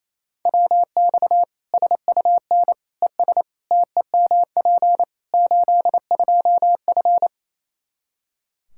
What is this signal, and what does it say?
Morse code, heard as a keyed tone, about 28 wpm, 700 Hz. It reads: WX SUN ES TEMP 82F
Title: K1USN SST QSO Practice 28wpm